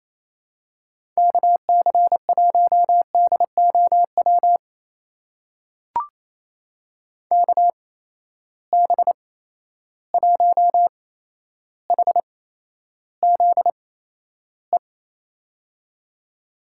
Morse code, heard as a keyed tone, about 28 wpm, spaced as for 12 wpm, 700 Hz. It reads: KC1DOW KB1HZE
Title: USA Call Signs - 6 Characters - 8x Character Spacing - 28wpm